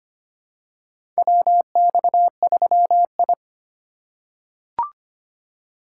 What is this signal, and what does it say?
Morse code, heard as a keyed tone, about 25 wpm, 700 Hz. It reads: WX3I
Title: USA Call Signs - 4 Characters - 2x Character Spacing - 25wpm